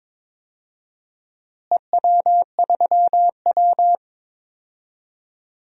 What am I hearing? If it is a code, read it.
Morse code, heard as a keyed tone, about 22 wpm, 700 Hz. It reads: EW3W
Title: DXCC-100 Call Signs - 4 Characters - Repeated 2x - 22wpm